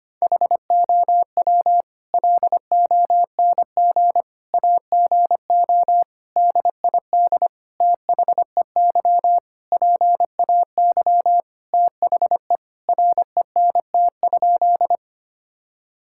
Morse code, heard as a keyed tone, about 25 wpm, 700 Hz. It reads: HOW LONG AGO DID THEY PAY THE RENT?
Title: Sentences from Top 1000 Words - Review Part 5 - 25wpm